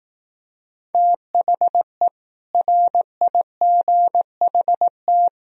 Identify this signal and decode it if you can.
Morse code, heard as a keyed tone, about 18 wpm, 700 Hz. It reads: THE RIGHT
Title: Speed-Racing - Sets of 5 Words 15wpm